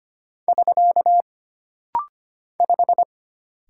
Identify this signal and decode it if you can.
Morse code, heard as a keyed tone, about 25 wpm, 700 Hz. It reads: <SK> 5
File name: Single Character (Letter:Number:Punctuation:Prosign) - Rapid-Fire - 25wpm